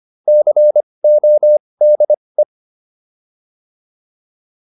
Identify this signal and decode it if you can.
Morse code, heard as a keyed tone, about 25 wpm, spaced as for 20 wpm, 600 Hz. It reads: CODE
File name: PR304 20